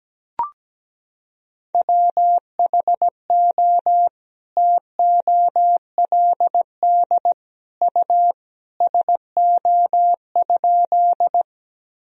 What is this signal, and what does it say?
Morse code, heard as a keyed tone, about 17 wpm, 700 Hz. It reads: WHO TOLD U SO?
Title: Sentences from Top 400 Words and Abbreviations 17wpm